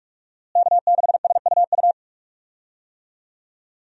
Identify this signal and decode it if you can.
Morse code, heard as a keyed tone, about 47 wpm, 700 Hz. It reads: K6SUU
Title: USA Call Signs - 5 Characters - Repeated 3x - 45wpm